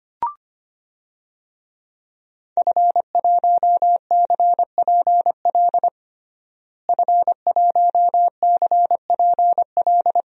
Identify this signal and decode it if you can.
Morse code, heard as a keyed tone, about 25 wpm, 700 Hz. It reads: F1CPL F1CPL
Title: DXCC-100 Call Signs - 5 Characters - Repeated 2x - 25wpm